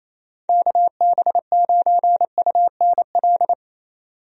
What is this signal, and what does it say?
Morse code, heard as a keyed tone, about 28 wpm, 700 Hz. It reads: KB9UNL
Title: USA Call Signs - 6 Characters - 28wpm